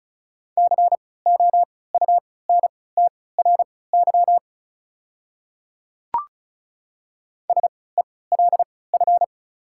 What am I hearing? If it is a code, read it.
Morse code, heard as a keyed tone, about 34 wpm, spaced as for 20 wpm, 690 Hz. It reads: COUNTRY SELF
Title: 3x Character Spacing - Top 300 Words - 35wpm